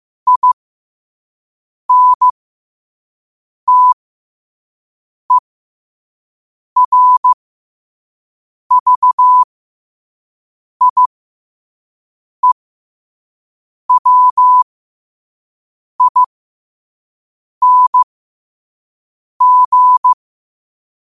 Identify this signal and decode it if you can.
Morse code, heard as a keyed tone, about 15 wpm, spaced as for 5 wpm, 1000 Hz. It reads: INTERVIEWING